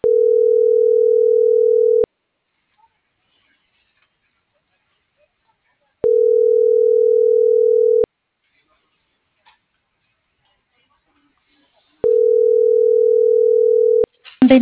Ringback tone (North America), 440 + 480 Hz.